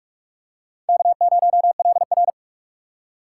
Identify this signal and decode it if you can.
Morse code, heard as a keyed tone, about 37 wpm, 700 Hz. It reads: K0LR
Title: Speed Racing - USA Call Signs 30wpm